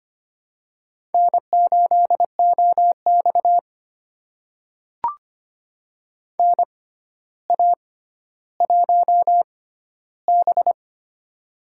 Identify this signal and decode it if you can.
Morse code, heard as a keyed tone, about 25 wpm, spaced as for 13 wpm, 700 Hz. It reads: N8OX NA1B
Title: USA Call Signs - 4 Characters - 6x Character Spacing - 25wpm